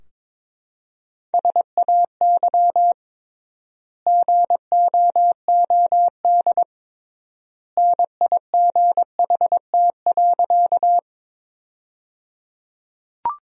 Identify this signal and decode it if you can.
Morse code, heard as a keyed tone, about 21 wpm, 700 Hz. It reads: SAY GOOD NIGHT.
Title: 3x Word-Spacing - Sentences from Top 300 Words - 22wpm